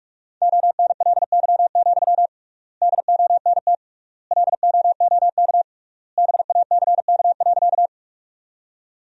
Morse code, heard as a keyed tone, about 46 wpm, 700 Hz. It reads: ONLY, DONT LOOK BACK.
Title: 3x Word-Spacing - Sentences from Top 300 Words - 45wpm